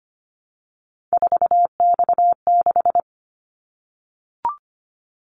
Morse code, heard as a keyed tone, about 25 wpm, 700 Hz. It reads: 4X6
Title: Callsign Prefixes for Top 50 DXCC - 25wpm